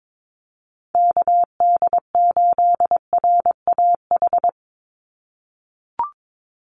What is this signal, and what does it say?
Morse code, heard as a keyed tone, about 22 wpm, 700 Hz. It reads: KD8RAH
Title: USA Call Signs 22wpm